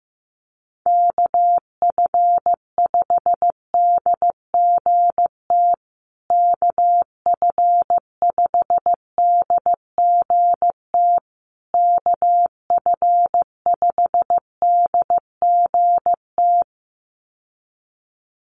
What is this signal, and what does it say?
Morse code, heard as a keyed tone, about 15 wpm, 700 Hz. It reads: KF5DGT KF5DGT KF5DGT
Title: USA Call Signs Repeated 3x 15wpm